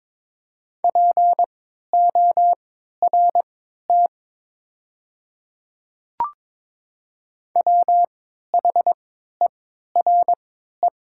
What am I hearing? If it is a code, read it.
Morse code, heard as a keyed tone, about 22 wpm, spaced as for 12 wpm, 700 Hz. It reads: PORT WHERE
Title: 3x Character Spacing - Top 200 Words - 22wpm